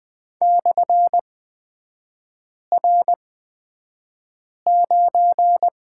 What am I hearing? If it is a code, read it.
Morse code, heard as a keyed tone, about 20 wpm, spaced as for 5 wpm, 690 Hz. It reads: /R9
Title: 229 Lesson 37e - Letter x - Three Characters 20wpm